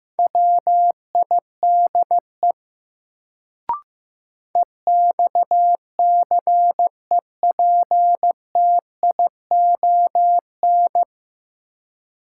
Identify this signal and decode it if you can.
Morse code, heard as a keyed tone, about 15 wpm, 700 Hz. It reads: WIDE EXCEPTION
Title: Sets of 4 Words - New Word Drill - 15wpm